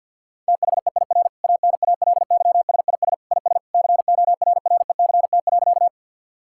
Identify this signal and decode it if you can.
Morse code, heard as a keyed tone, about 50 wpm, 700 Hz. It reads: THEIR ANALYSIS IS CORRECT.
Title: Sentences from Top 1000 Words - Review Part 5 - 50wpm